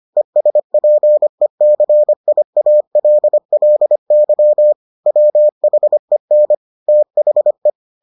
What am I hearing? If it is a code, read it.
Morse code, heard as a keyed tone, about 25 wpm, 600 Hz. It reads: ESPECIALLY WHEN THE